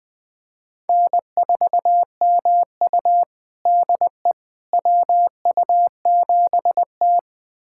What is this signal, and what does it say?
Morse code, heard as a keyed tone, about 21 wpm, 700 Hz. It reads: N4MU DE WU7T
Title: USA Callsign DE Callsign - 4 Characters - 20wpm